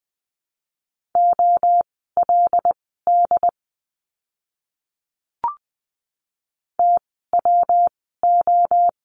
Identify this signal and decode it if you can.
Morse code, heard as a keyed tone, about 20 wpm, spaced as for 14 wpm, 700 Hz. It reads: OLD TWO
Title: 192 Lesson 31f - Letter v - Words 20wpm